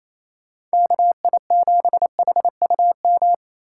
Morse code, heard as a keyed tone, about 35 wpm, 700 Hz. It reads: KI7HUM